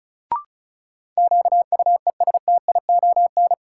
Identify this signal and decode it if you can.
Morse code, heard as a keyed tone, about 36 wpm, 700 Hz. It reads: QUESTION
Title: Sets of 5 Words Encore - New Word Drill - 35wpm